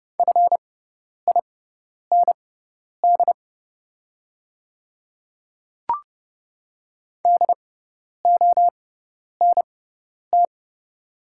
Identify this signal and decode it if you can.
Morse code, heard as a keyed tone, about 30 wpm, spaced as for 10 wpm, 690 Hz. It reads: FIND DONT
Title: 6x Character Spacing - Top 500 Words - 30wpm